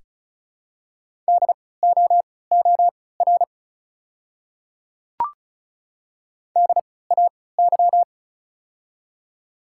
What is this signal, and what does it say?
Morse code, heard as a keyed tone, about 34 wpm, spaced as for 20 wpm, 700 Hz. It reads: DOOR DAY
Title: 3x Character Spacing - Top 300 Words - 35wpm